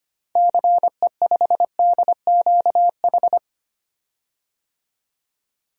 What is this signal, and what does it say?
Morse code, heard as a keyed tone, about 25 wpm, 700 Hz. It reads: CE5DQH
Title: DXCC-100 Call Signs - 6 Characters - 25wpm